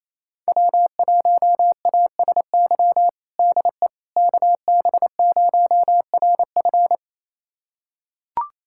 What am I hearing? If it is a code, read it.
Morse code, heard as a keyed tone, about 28 wpm, 700 Hz. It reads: W1ASY DE KB0RF
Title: USA Callsign DE Callsign - 5 Characters - 28wpm